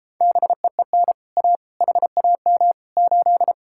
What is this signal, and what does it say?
Morse code, heard as a keyed tone, about 37 wpm, 700 Hz. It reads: BEEN A HAM 8